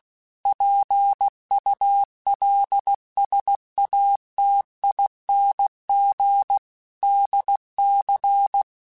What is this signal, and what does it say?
Morse code, heard as a keyed tone, about 16 wpm, 800 Hz. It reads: PULSATING DC
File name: Technical Subjects / Technical Subjects-mp3s / Sw Pwr Sup @15